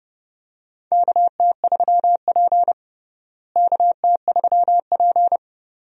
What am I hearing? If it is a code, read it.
Morse code, heard as a keyed tone, about 30 wpm, 700 Hz. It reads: KT3P KT3P
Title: USA Call Signs - 4 Characters - Repeated 2x - 30wpm